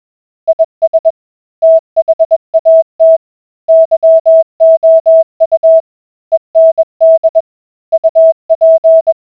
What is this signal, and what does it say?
Morse code, heard as a keyed tone, about 20 wpm, 650 Hz. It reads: IS THAT YOU END UP